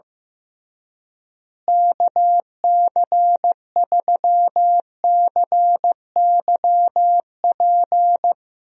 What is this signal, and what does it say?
Morse code, heard as a keyed tone, about 15 wpm, 700 Hz. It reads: KC3CYP
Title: USA Call Signs - 6 Characters - Repeated 3x - 15wpm